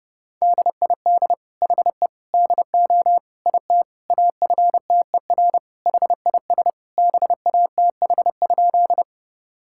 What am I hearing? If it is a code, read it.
Morse code, heard as a keyed tone, about 30 wpm, 700 Hz. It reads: DID HE DO IT AFTER HIS BATH?
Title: Sentences from Top 1600 Words 30wpm